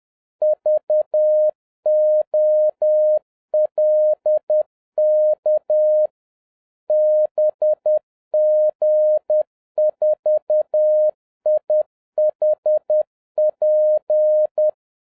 Morse code, heard as a keyed tone, about 10 wpm, 610 Hz. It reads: VOLK BG4IHP